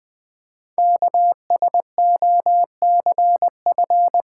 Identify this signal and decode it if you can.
Morse code, heard as a keyed tone, about 20 wpm, 700 Hz. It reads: KSOCF
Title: Code Groups - 5 Letter:Numbers - 20wpm